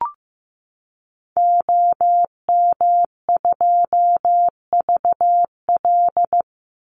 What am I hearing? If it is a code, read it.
Morse code, heard as a keyed tone, about 15 wpm, 700 Hz. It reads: OM2VL
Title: Call Signs Repeated 3x - 15wpm